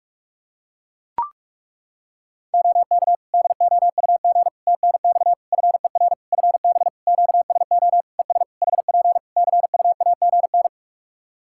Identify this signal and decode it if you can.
Morse code, heard as a keyed tone, about 41 wpm, 700 Hz. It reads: OK DOUG TNX FER FB QSO ES HP CUAGN
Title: Practice QSO - Exchange 4 - Ending - 45wpm